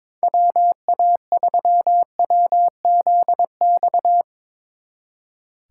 Morse code, heard as a keyed tone, about 25 wpm, 700 Hz. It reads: WA3WZX
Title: Speed Racing - USA 2x3 Call Signs 22wpm